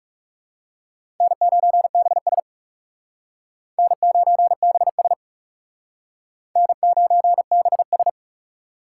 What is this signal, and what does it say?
Morse code, heard as a keyed tone, about 40 wpm, 700 Hz. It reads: N9BS N9BS N9BS